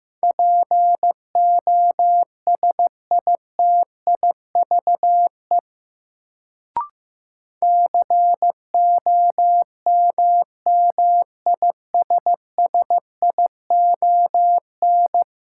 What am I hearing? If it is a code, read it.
Morse code, heard as a keyed tone, about 15 wpm, 700 Hz. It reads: POSITIVE COMMISSION
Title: Sentences from Top 1400 Words - New Word Drill Alt - 15wpm